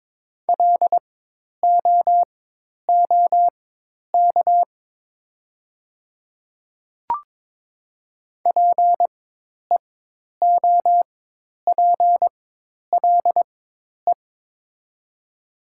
Morse code, heard as a keyed tone, about 22 wpm, spaced as for 10 wpm, 700 Hz. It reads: LOOK PEOPLE
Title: 4x Character Spacing - Top 100 Words - 22wpm